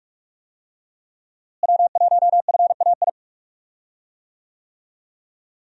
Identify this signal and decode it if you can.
Morse code, heard as a keyed tone, about 41 wpm, 700 Hz. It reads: W1FAI
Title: USA Call Signs - 5 Characters - Repeated 2x - 45wpm